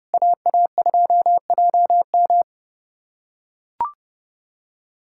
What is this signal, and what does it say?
Morse code, heard as a keyed tone, about 32 wpm, 700 Hz. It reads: AA2JM
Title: USA Callsign DE Callsign - 5 Characters - 30wpm